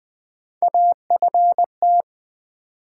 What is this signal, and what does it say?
Morse code, heard as a keyed tone, about 20 wpm, 700 Hz. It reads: AFT
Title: Build A Word - 3 Letters At A Time - 9-Letter Words - 20wpm